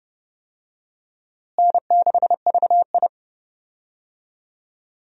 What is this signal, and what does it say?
Morse code, heard as a keyed tone, about 30 wpm, 700 Hz. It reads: N6VI
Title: USA Call Signs - 4 Characters - Repeated 2x - 30wpm